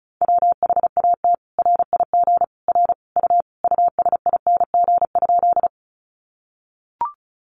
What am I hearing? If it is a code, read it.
Morse code, heard as a keyed tone, about 34 wpm, 700 Hz. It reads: WHAT RIG R U USING?
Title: Common QSO Phrases 35wpm